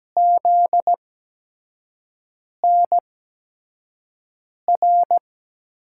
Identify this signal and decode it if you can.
Morse code, heard as a keyed tone, about 17 wpm, spaced as for 5 wpm, 700 Hz. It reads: ZNR
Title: USA Call Signs - 5 Characters - 8x Character Spacing - 17wpm